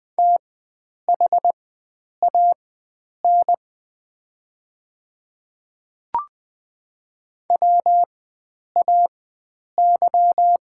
Morse code, heard as a keyed tone, about 20 wpm, spaced as for 9 wpm, 700 Hz. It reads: THAN WAY
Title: 4x Character Spacing - Top 100 Words - 20wpm